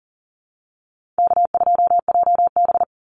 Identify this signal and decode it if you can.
Morse code, heard as a keyed tone, about 40 wpm, 700 Hz. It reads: K2JB
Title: NA SOTA Activator Callsigns 40wpm